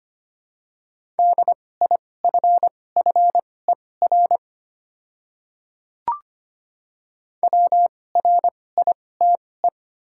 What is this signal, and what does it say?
Morse code, heard as a keyed tone, about 25 wpm, spaced as for 18 wpm, 700 Hz. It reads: DIFFER WRITE
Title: 2x Character Spacing - Top 200 Words - 25wpm